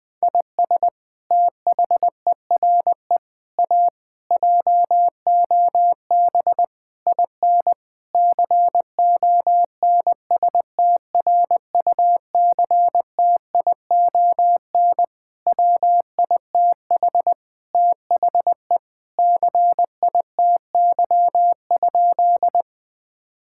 Morse code, heard as a keyed tone, about 20 wpm, 700 Hz. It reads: IS THERE A JOB IN CONSTRUCTION WITH THE CITY?
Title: Sentences from Top 1600 Words 20wpm